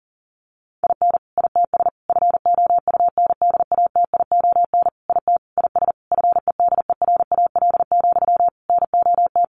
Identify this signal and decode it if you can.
Morse code, heard as a keyed tone, about 40 wpm, 700 Hz. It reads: IN ITS FOUNDATION IT IS FEDERAL, NOT
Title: Sentences from Top 1200 Words 40wpm